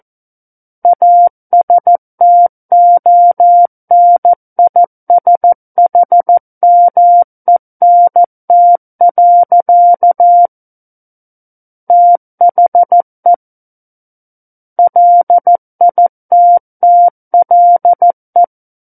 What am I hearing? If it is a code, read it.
Morse code, heard as a keyed tone, about 14 wpm, 700 Hz. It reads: ASTONISHMENT. THE LITTLE
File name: Book One / 05-Chapter 5 - 14 WPM